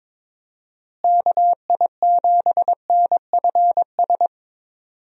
Morse code, heard as a keyed tone, about 21 wpm, 700 Hz. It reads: KI7NFS
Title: USA Call Signs - 6 Characters - Repeated 2x - 22wpm